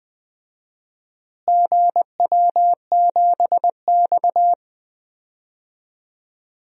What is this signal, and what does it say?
Morse code, heard as a keyed tone, about 20 wpm, 700 Hz. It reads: GW7X
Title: DXCC-100 Call Signs - 4 Characters - Repeated 3x - 20wpm